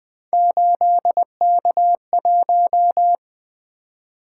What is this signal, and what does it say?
Morse code, heard as a keyed tone, about 20 wpm, 700 Hz. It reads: 8K1
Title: Three Letter-Number Combinations - 20wpm